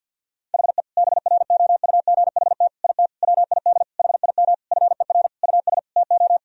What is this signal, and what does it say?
Morse code, heard as a keyed tone, about 46 wpm, 700 Hz. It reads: HE BROUGHT IT WID HIM FER US TO
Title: Sentences from Top 500 Words and Abbreviations 50wpm